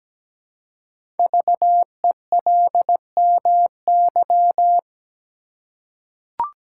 Morse code, heard as a keyed tone, about 17 wpm, 700 Hz. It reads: VELMY
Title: Code Groups - 5 Letters - 17wpm